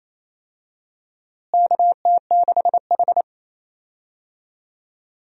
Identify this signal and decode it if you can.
Morse code, heard as a keyed tone, about 28 wpm, 700 Hz. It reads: KT6H
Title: USA Call Signs - 4 Characters - Repeated 2x - 28wpm